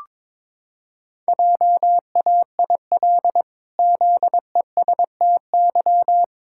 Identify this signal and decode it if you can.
Morse code, heard as a keyed tone, about 22 wpm, 700 Hz. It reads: JAIL ZESTY
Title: Pangram Challenge v2 - 22wpm